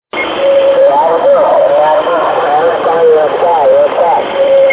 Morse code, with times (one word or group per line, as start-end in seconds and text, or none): none